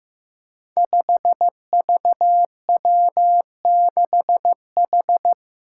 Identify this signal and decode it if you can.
Morse code, heard as a keyed tone, about 15 wpm, 700 Hz. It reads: 5VW6H
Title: Five Challenging Letter-Number Combinations 15wpm